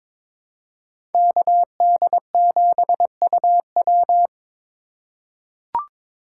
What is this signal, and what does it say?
Morse code, heard as a keyed tone, about 22 wpm, 700 Hz. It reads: KD7UW
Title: USA Call Signs - 5 Characters - 3x Character Spacing - 22wpm